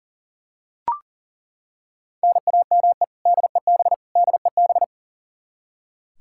Morse code, heard as a keyed tone, about 40 wpm, 700 Hz. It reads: NAME DEB DEB
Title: Practice QSO - Exchange 1 - Beginning - 40wpm